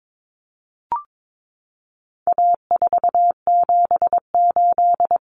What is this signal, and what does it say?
Morse code, heard as a keyed tone, about 22 wpm, 700 Hz. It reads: A478
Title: Common Cut Numbers - 4-Digit 22wpm